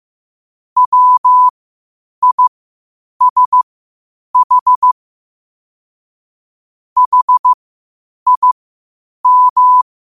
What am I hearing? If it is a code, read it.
Morse code, heard as a keyed tone, about 15 wpm, spaced as for 8 wpm, 1000 Hz. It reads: WISH HIM
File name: qotdfiles / 8wpm